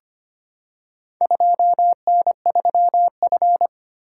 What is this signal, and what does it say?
Morse code, heard as a keyed tone, about 25 wpm, 700 Hz. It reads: 2N3F
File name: practice / 25wpm / Four Letter-Number Combinations - 25wpm